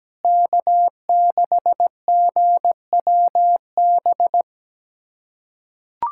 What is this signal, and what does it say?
Morse code, heard as a keyed tone, about 17 wpm, 700 Hz. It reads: K6GWB
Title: USA Callsign DE Callsign - 5 Characters - 17wpm